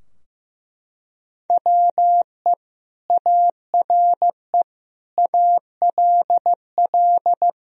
Morse code, heard as a keyed tone, about 15 wpm, 700 Hz. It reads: WE ARE ALL
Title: Sentences from Top 1000 Words - Review Part 3 - 15wpm